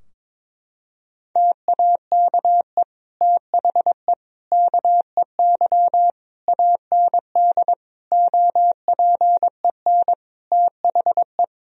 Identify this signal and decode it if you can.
Morse code, heard as a keyed tone, about 22 wpm, 700 Hz. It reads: TAKE THE KEY AND OPEN THE
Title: Sentences from Top 1000 Words - Review Part 4 - 22wpm